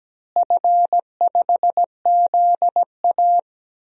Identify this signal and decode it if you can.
Morse code, heard as a keyed tone, about 17 wpm, 700 Hz. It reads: F5ZA
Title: DXCC-100 Call Signs - 4 Characters - Repeated 2x - 17wpm